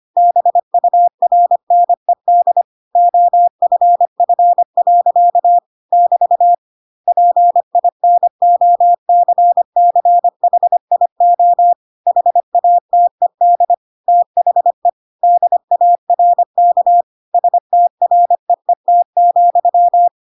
Morse code, heard as a keyed tone, about 25 wpm, 700 Hz. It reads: BURNED OFF. <BT> PINOCCHIO HATED THE DARK STREET,
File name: Pinocchio / Pinocchio ch6 25 25 ik1hge